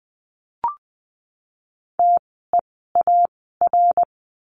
Morse code, heard as a keyed tone, about 20 wpm, spaced as for 14 wpm, 700 Hz. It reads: TEAR
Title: 068 Lesson 12f - Letter d - Words 20wpm